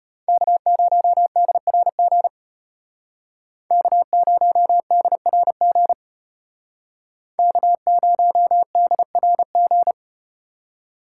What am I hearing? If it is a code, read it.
Morse code, heard as a keyed tone, about 33 wpm, 700 Hz. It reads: K0DRG K0DRG K0DRG